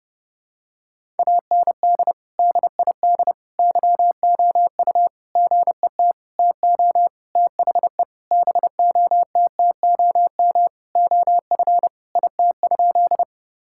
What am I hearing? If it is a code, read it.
Morse code, heard as a keyed tone, about 30 wpm, 700 Hz. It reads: AND DID YOU GET TO THE BOTTOM OF IT?
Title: Sentences from Top 1000 Words - Review Part 6 - 30wpm